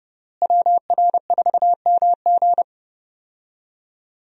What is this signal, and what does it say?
Morse code, heard as a keyed tone, about 30 wpm, 700 Hz. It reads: WR4MG
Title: USA Call Signs - 5 Characters - Repeated 3x - 30wpm